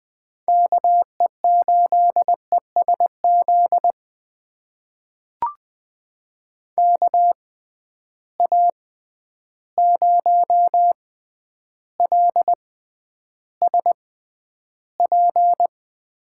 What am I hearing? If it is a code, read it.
Morse code, heard as a keyed tone, about 20 wpm, spaced as for 10 wpm, 700 Hz. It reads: KE8ESZ KA0LSP
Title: USA Call Signs - 6x Character Spacing - 20wpm